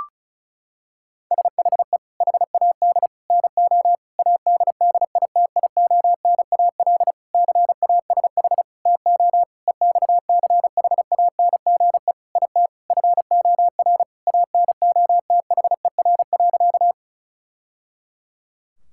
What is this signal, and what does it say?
Morse code, heard as a keyed tone, about 34 wpm, 700 Hz. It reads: SHE HAD NO ADDITIONAL CASH TO EXCHANGE IT FOR ANOTHER.
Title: Sentences from Top 1000 Words - Review Part 6 - 35wpm